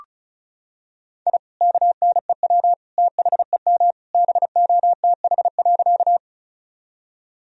Morse code, heard as a keyed tone, about 36 wpm, 700 Hz. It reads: I KNEW THEM BOTH.
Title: Sentences from Top 1000 Words - Review Part 1 - 35wpm